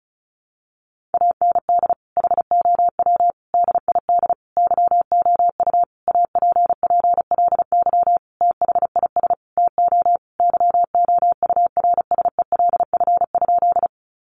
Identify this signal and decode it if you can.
Morse code, heard as a keyed tone, about 34 wpm, 700 Hz. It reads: AND HOW DID YOU APPLY THIS TO YOURSELF?
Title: Sentences from Top 1200 Words 35wpm